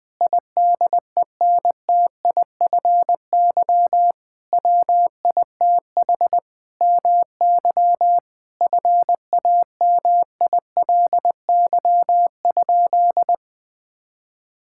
Morse code, heard as a keyed tone, about 20 wpm, 700 Hz. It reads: IDENTIFY WITH MY FAMILY?
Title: Sentences from Top 1000 Words - Review Part 6 - 20wpm